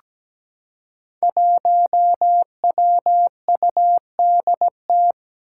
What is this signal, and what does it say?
Morse code, heard as a keyed tone, about 17 wpm, 700 Hz. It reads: 1WUDT
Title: Five Challenging Letter-Number Combinations 17wpm